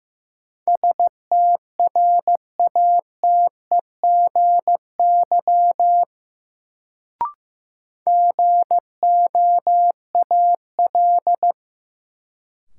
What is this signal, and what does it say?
Morse code, heard as a keyed tone, about 16 wpm, 700 Hz. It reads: STRATEGY GOAL
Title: Sentences from Top 1000 Words - Review Part 5 - New Word Drill - 15wpm